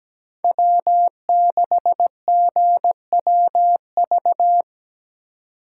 Morse code, heard as a keyed tone, about 20 wpm, 700 Hz. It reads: W6GWV
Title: Speed Racing - USA 1x3 Call Signs 17wpm